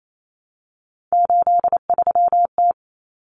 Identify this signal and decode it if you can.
Morse code, heard as a keyed tone, about 28 wpm, 700 Hz. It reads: 83T
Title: Common Cut Numbers - 3-Digit 28wpm